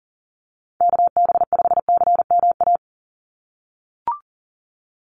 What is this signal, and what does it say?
Morse code, heard as a keyed tone, about 37 wpm, 700 Hz. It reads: KB5CMA
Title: USA Call Signs 40wpm